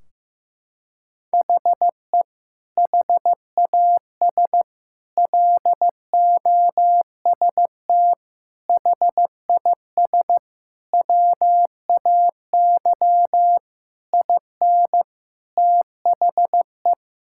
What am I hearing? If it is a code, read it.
Morse code, heard as a keyed tone, about 16 wpm, 700 Hz. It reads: HE HAS LOST HIS WAY IN THE
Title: Sentences from Top 1600 Words 15wpm